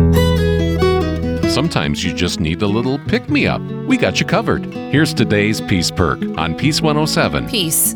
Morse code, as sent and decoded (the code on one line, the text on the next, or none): none